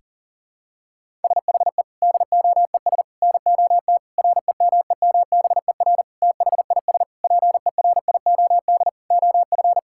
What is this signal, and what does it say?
Morse code, heard as a keyed tone, about 40 wpm, 700 Hz. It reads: SHE DOES NOT REMEMBER THIS PERIOD OF
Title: Sentences from Top 1000 Words - Review Part 5 - 40wpm